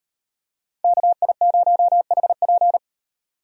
Morse code, heard as a keyed tone, about 35 wpm, 700 Hz. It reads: KI0HP